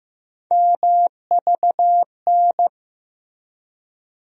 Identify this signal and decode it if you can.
Morse code, heard as a keyed tone, about 15 wpm, 700 Hz. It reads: MVN
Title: USA Call Sign Postfixes - 3 Letters 15wpm